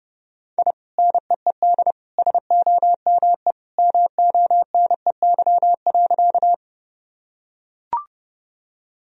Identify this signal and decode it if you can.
Morse code, heard as a keyed tone, about 30 wpm, 700 Hz. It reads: I NEED SOME MONEY.
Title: Sentences from Top 1000 Words - Review Part 1 - 30wpm